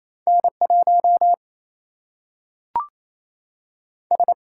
Morse code, heard as a keyed tone, about 28 wpm, 700 Hz. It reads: N 1 S
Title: Single-Letter Single-Number 28wpm